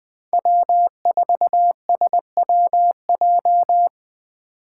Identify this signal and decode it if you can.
Morse code, heard as a keyed tone, about 22 wpm, 700 Hz. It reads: W4SWJ